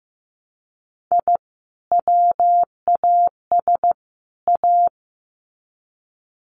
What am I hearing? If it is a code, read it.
Morse code, heard as a keyed tone, about 15 wpm, 700 Hz. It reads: I WAS A
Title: Sets of 3 Words 15wpm